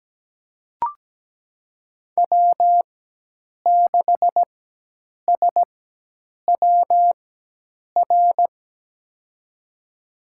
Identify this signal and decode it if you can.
Morse code, heard as a keyed tone, about 17 wpm, spaced as for 8 wpm, 700 Hz. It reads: W6SWR
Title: USA Call Signs - 5 Characters - 4x Character Spacing - 17wpm